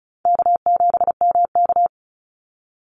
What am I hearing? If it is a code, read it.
Morse code, heard as a keyed tone, about 34 wpm, 700 Hz. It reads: K7MK
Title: NA SOTA Activator Callsigns 35wpm